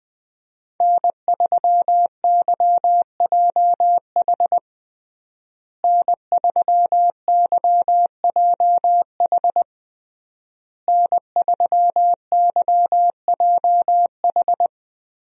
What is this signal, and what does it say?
Morse code, heard as a keyed tone, about 21 wpm, 700 Hz. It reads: N3YJH N3YJH N3YJH
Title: USA Call Signs - 5 Characters - Repeated 3x - 20wpm